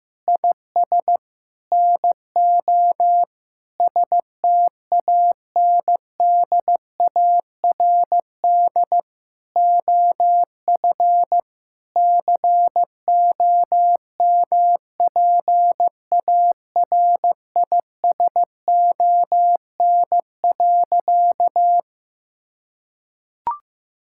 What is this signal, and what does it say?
Morse code, heard as a keyed tone, about 16 wpm, 700 Hz. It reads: IS NO STANDARD OF COMPARISON.
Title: Sentences from Top 1600 Words 15wpm